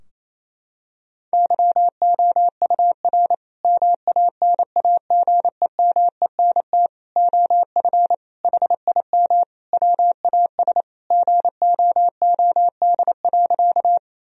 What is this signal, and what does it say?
Morse code, heard as a keyed tone, about 28 wpm, 700 Hz. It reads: YOUR MANAGEMENT OF HIM WAS GOOD.
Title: Sentences from Top 1000 Words - Review Part 4 - 28wpm